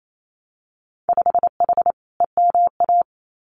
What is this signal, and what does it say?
Morse code, heard as a keyed tone, about 28 wpm, 700 Hz. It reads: 5H EMA
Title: Winter Field Day Exchanges Encore 28wpm